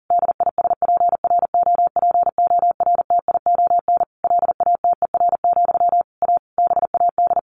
Morse code, heard as a keyed tone, about 40 wpm, 700 Hz. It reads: DISPROPORTION LATER, A BAD
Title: A Sound Of Thunder 40wpm